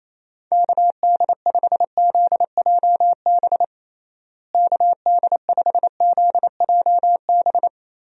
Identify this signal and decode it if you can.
Morse code, heard as a keyed tone, about 28 wpm, 700 Hz. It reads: KD5ZJB KD5ZJB
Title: USA Call Signs - 6 Characters - Repeated 3x - 28wpm